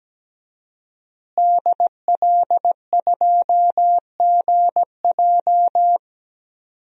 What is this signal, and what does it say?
Morse code, heard as a keyed tone, about 17 wpm, 700 Hz. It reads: DL2GJ
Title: DXCC-100 Call Signs - 5 Characters - Repeated 2x - 17wpm